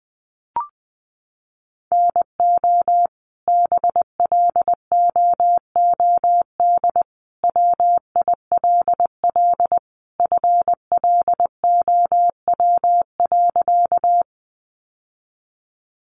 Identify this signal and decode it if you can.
Morse code, heard as a keyed tone, about 20 wpm, 700 Hz. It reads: NO BLOOD WILL FLOW.
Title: Sentences from Top 1200 Words 20wpm